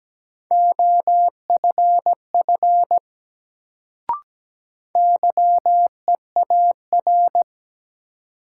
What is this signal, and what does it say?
Morse code, heard as a keyed tone, about 17 wpm, 700 Hz. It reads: OFF YEAR
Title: Sentences from Top 200 Words - New Word Drill - 17wpm